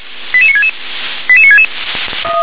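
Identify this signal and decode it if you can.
Five-tone selective call (ZVEI), heard as quick runs of five tones, tones of about 80 ms. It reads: 88966 88966
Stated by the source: ZVEI five-tone selective call received by SDR.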